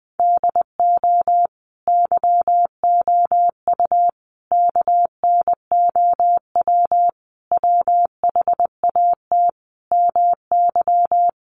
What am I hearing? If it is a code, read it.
Morse code, heard as a keyed tone, about 20 wpm, 700 Hz. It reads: DO YOU KNOW WHAT MY
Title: Sentences from Top 1200 Words 20wpm